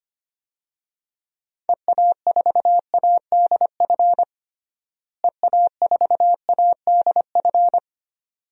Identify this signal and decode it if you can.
Morse code, heard as a keyed tone, about 25 wpm, 700 Hz. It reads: EA4ADF EA4ADF
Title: DXCC-100 Call Signs - 6 Characters - Repeated 2x - 25wpm